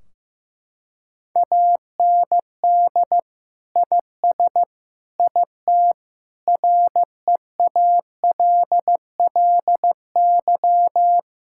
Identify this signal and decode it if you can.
Morse code, heard as a keyed tone, about 15 wpm, 700 Hz. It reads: AND IS IT REALLY
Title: Sentences from Top 1600 Words 15wpm